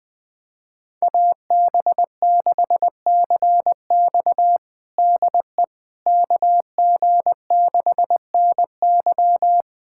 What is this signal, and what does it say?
Morse code, heard as a keyed tone, about 20 wpm, 700 Hz. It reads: AB6CX DE KG6NY
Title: USA Callsign DE Callsign - 5 Characters - 20wpm